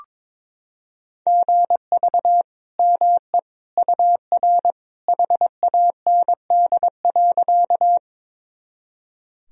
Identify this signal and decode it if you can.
Morse code, heard as a keyed tone, about 22 wpm, 700 Hz. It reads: GV ME UR HAND.
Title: Sentences from Top 200 Words and Abbreviations 22wpm